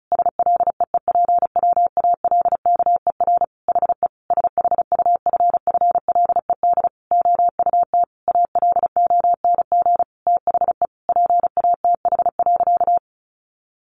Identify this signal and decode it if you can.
Morse code, heard as a keyed tone, about 34 wpm, 700 Hz. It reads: SLEEPWALKER HE SHUFFLED OUT ALONG THE PATH.
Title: A Sound Of Thunder 35wpm